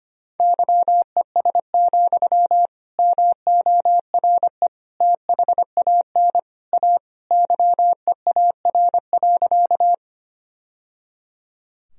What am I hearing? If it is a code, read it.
Morse code, heard as a keyed tone, about 25 wpm, 700 Hz. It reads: YES, MORE THAN A YEAR.
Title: Sentences from Top 1000 Words - Review Part 1 - 25wpm